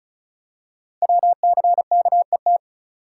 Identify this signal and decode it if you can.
Morse code, heard as a keyed tone, about 36 wpm, 700 Hz. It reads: WCKET
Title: Code Groups - 5 Letters - 35wpm